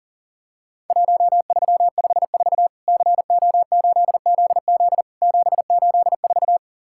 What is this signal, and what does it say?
Morse code, heard as a keyed tone, about 40 wpm, 700 Hz. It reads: 1354 CO8ZZ 784
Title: Worked All Europe DX Contest - QTC Exchange - 40wpm